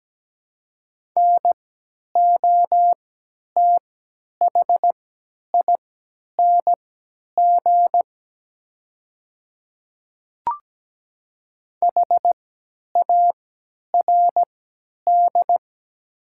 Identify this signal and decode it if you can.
Morse code, heard as a keyed tone, about 17 wpm, spaced as for 10 wpm, 700 Hz. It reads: NOTHING HARD
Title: 3x Character Spacing - Top 500 Words - 17wpm